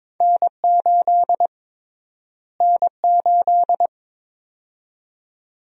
Morse code, heard as a keyed tone, about 21 wpm, 700 Hz. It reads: N8 N8
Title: USA Call Sign Prefixes - Repeated 2x - 22wpm